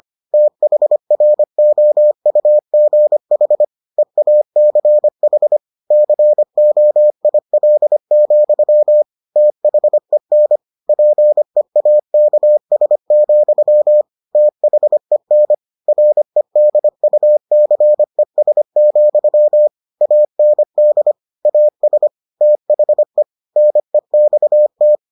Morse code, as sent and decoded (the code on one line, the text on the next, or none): - .... .-. --- ..- --. .... / . .- -.-. .... / -.-. --- .. .-.. --..-- / - .... -. / .--. .- -.- ... --..-- / - .... -. / .-. -.. ..- -.-. ... --..-- / .- -. -.. / .- ... / - .... / -. -..- -
THROUGH EACH COIL, THEN PEAKS, THEN REDUCES, AND AS THE NEXT